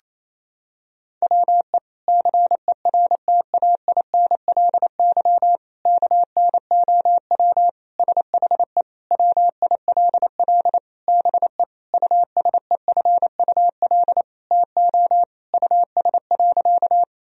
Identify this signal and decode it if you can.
Morse code, heard as a keyed tone, about 28 wpm, 700 Hz. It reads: WE CERTAINLY KNOW SHE WILL BE USEFUL TO US.
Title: Sentences from Top 1000 Words - Review Part 4 - 28wpm